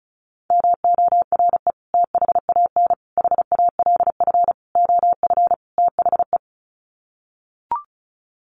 Morse code, heard as a keyed tone, about 34 wpm, 700 Hz. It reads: MORE THAN HALF OF THE
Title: Sets of 5 Words 35wpm